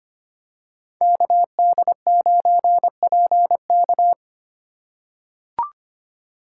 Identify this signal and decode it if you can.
Morse code, heard as a keyed tone, about 25 wpm, 700 Hz. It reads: KD9PK
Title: USA Call Signs - 5 Characters - 3x Character Spacing - 25wpm